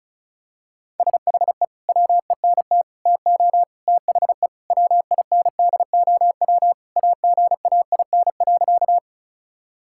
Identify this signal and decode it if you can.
Morse code, heard as a keyed tone, about 36 wpm, 700 Hz. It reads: SHE WENT TO THE WINDOW AGAIN.
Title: Sentences from Top 1600 Words 35wpm